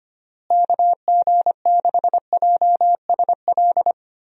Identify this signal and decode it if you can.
Morse code, heard as a keyed tone, about 25 wpm, 700 Hz. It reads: KG6JSL
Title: USA Call Signs - 6 Characters - Repeated 3x - 25wpm